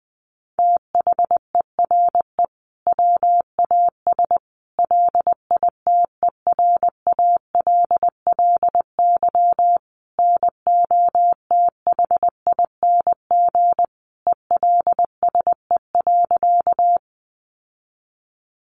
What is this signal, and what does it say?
Morse code, heard as a keyed tone, about 20 wpm, 700 Hz. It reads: THERE WAS LITERALLY NOTHING ELSE.
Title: Sentences from Top 1400 Words 20wpm